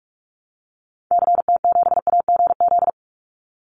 Morse code, heard as a keyed tone, about 44 wpm, 700 Hz. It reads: CT7AGZ
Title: Call Signs Repeated 2x - 45wpm